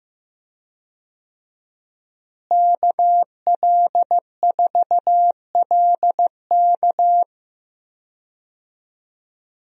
Morse code, heard as a keyed tone, about 15 wpm, 700 Hz. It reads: KL4LK
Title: USA Call Signs - 5 Characters - Repeated 3x - 15wpm